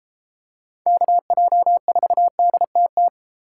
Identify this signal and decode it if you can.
Morse code, heard as a keyed tone, about 32 wpm, 700 Hz. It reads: KJ4DTT